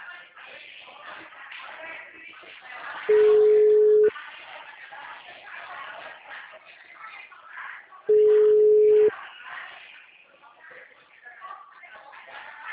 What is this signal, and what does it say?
Ringback tone (Europe, 425 Hz).